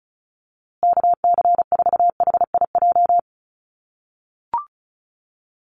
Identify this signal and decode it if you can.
Morse code, heard as a keyed tone, about 34 wpm, 700 Hz. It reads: KC4HIJ
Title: USA Call Signs Repeated 2x 35wpm